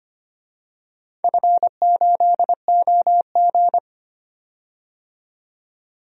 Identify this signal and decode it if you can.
Morse code, heard as a keyed tone, about 25 wpm, 700 Hz. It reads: F8OG
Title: DXCC-100 Call Signs - 4 Characters - 25wpm